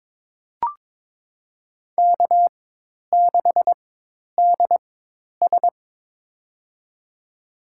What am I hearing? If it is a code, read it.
Morse code, heard as a keyed tone, about 21 wpm, spaced as for 10 wpm, 700 Hz. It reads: K6DS
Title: USA Call Signs - 4 Characters - 4x Character Spacing - 22wpm